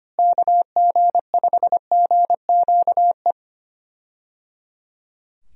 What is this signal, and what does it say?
Morse code, heard as a keyed tone, about 25 wpm, 700 Hz. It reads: KG5GQE
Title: USA Call Signs - 6 Characters - Repeated 3x - 25wpm